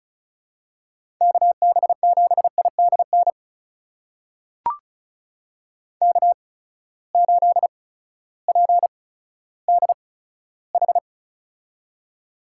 Morse code, heard as a keyed tone, about 36 wpm, spaced as for 16 wpm, 700 Hz. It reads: KB7IDN K8PDH
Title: USA Call Signs - 8x Character Spacing - 35wpm